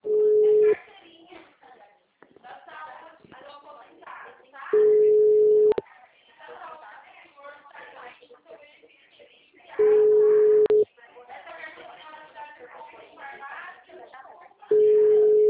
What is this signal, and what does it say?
Ringback tone (Europe, 425 Hz), with speech.